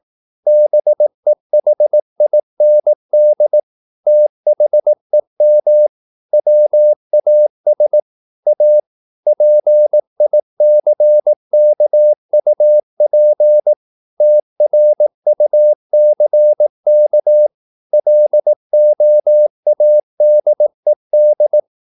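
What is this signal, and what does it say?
Morse code, heard as a keyed tone, about 18 wpm, 600 Hz. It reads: BEHIND THEM WAS A PICKUP TRUCK LOADED
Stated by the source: Morse practice at 18 wpm, English text, from CW med Gnister.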